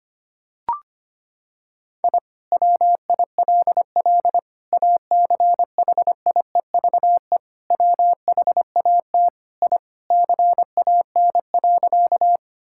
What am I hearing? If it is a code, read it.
Morse code, heard as a keyed tone, about 25 wpm, 700 Hz. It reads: I WILL ACHIEVE WHAT I CAN.
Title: Sentences from Top 1000 Words - Review Part 6 - 25wpm